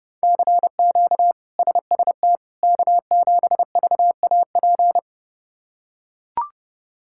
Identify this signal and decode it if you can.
Morse code, heard as a keyed tone, about 31 wpm, 700 Hz. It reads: CQ SST K7VAP
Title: K1USN SST QSO Practice - 30wpm